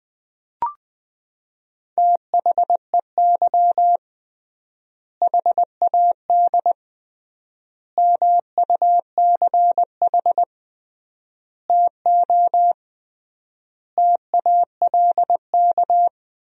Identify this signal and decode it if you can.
Morse code, heard as a keyed tone, about 20 wpm, 700 Hz. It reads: THEY HAD MUCH TO TALK
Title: 3x Word-Spacing - Sentences from Top 400 Words - 20wpm